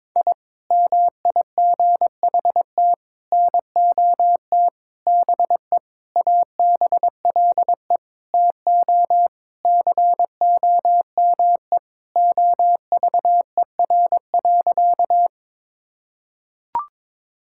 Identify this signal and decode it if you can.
Morse code, heard as a keyed tone, about 22 wpm, 700 Hz. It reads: I MIGHT NOT BE ABLE TO COME OVER.
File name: Sentences with 4-Word Phrases - 22wpm